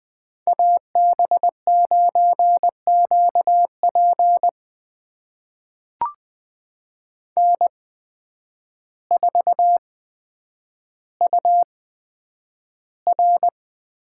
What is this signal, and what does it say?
Morse code, heard as a keyed tone, about 20 wpm, spaced as for 9 wpm, 700 Hz. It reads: AB9QP N4UR
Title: USA Call Signs - 5 Characters - 8x Character Spacing - 20wpm